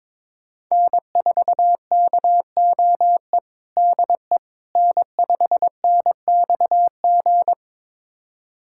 Morse code, heard as a keyed tone, about 22 wpm, 700 Hz. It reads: N4KOE DE N5NXG
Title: USA Callsign DE Callsign - 5 Characters - 22wpm